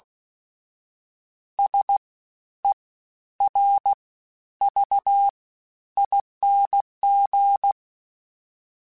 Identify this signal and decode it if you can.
Morse code, heard as a keyed tone, about 16 wpm, 800 Hz. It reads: S E R V ING
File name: Special INGs-mp3s / Special INGs-mp3s / Fast ing @15